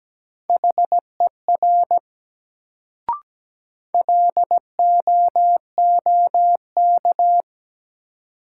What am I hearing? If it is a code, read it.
Morse code, heard as a keyed tone, about 17 wpm, 700 Hz. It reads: HER LOOK
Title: Sets of 2 Words - New Word Drill - 17wpm